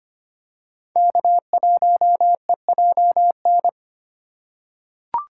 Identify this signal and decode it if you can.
Morse code, heard as a keyed tone, about 25 wpm, 700 Hz. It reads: K1EJN
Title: USA Call Signs - 5 Characters - 4x Character Spacing - 25wpm